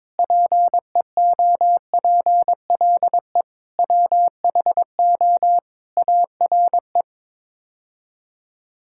Morse code, heard as a keyed tone, about 22 wpm, 700 Hz. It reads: PEOPLE WHO ARE
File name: practice / 22wpm / Sets of 8 Words - 22wpm